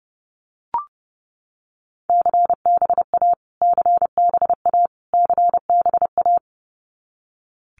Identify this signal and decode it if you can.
Morse code, heard as a keyed tone, about 30 wpm, 700 Hz. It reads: CBA CBA CBA
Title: Abbreviations Repeated 3x - 30wpm